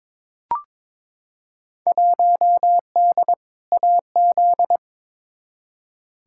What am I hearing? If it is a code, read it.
Morse code, heard as a keyed tone, about 22 wpm, 700 Hz. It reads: 1D AZ
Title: Field Day Exchanges - 22wpm